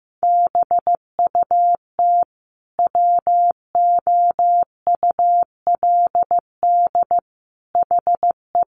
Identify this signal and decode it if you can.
Morse code, heard as a keyed tone, about 15 wpm, 700 Hz. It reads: BUT WOULD HE
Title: Sentences from Top 1400 Words 15wpm